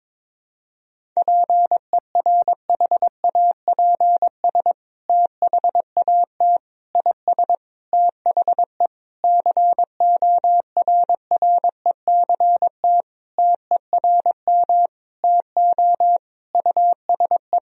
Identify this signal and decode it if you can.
Morse code, heard as a keyed tone, about 22 wpm, 700 Hz. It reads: PERHAPS THAT IS THE CORRECT TERM TO USE
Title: Sentences from Top 1000 Words - Review Part 4 - 22wpm